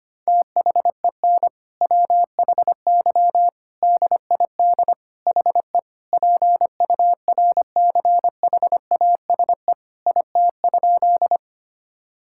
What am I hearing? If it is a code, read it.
Morse code, heard as a keyed tone, about 25 wpm, 700 Hz. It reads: THEN WHY DID HE PURCHASE IT?
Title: Sentences from Top 1000 Words - Review Part 5 - 25wpm